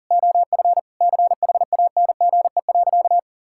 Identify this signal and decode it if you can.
Morse code, heard as a keyed tone, about 40 wpm, 700 Hz. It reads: OF CHANGE.
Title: Sentences from Top 1000 Words - Review Part 5 - 40wpm